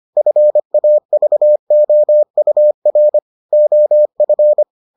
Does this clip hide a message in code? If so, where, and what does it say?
Yes: Morse code from 0.2 to 4.6 s, heard as a keyed tone, about 25 wpm, 600 Hz. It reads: FAVOUR OF